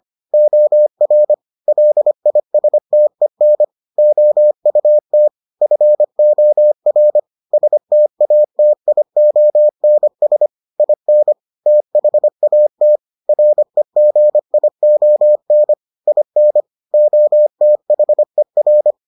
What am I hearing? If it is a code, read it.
Morse code, heard as a keyed tone, about 25 wpm, 600 Hz. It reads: OR LISTEN OUT FOR STATIONS IN THAT REGION IN OTHER